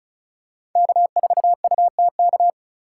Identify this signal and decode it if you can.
Morse code, heard as a keyed tone, about 34 wpm, 700 Hz. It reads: K4UTK
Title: USA Call Signs - 5 Characters - 4x Character Spacing - 35wpm